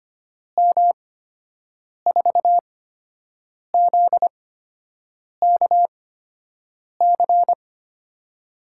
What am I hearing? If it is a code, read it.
Morse code, heard as a keyed tone, about 25 wpm, spaced as for 7 wpm, 700 Hz. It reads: M4ZKC
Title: USA Call Signs - 6 Characters - 8x Character Spacing - 25wpm